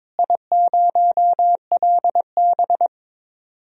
Morse code, heard as a keyed tone, about 22 wpm, 700 Hz. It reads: I0LB
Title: DXCC-100 Call Signs - 4 Characters - Repeated 3x - 22wpm